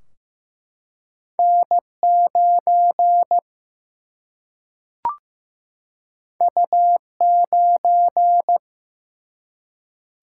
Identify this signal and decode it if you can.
Morse code, heard as a keyed tone, about 15 wpm, 700 Hz. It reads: N9 U9
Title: Single-Letter Single-Number 15wpm